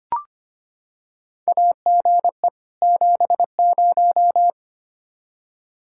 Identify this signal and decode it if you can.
Morse code, heard as a keyed tone, about 25 wpm, 700 Hz. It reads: AGE 70
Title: Practice QSO - Exchange 3 - More Info - 25wpm